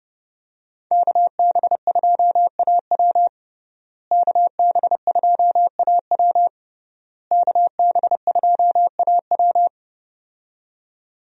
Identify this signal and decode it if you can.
Morse code, heard as a keyed tone, about 30 wpm, 700 Hz. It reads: KB2AW KB2AW KB2AW
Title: USA Call Signs - 5 Characters - Repeated 3x - 30wpm